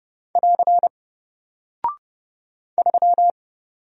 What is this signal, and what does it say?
Morse code, heard as a keyed tone, about 30 wpm, 700 Hz. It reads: <AR> 3
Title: Single Character (Letter:Number:Punctuation:Prosign) Alt 30wpm